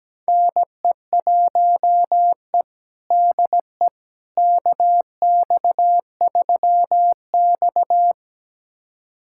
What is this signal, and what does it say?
Morse code, heard as a keyed tone, about 17 wpm, 700 Hz. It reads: NE1E DE KX3X
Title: USA Callsign DE Callsign - 4 Characters - 17wpm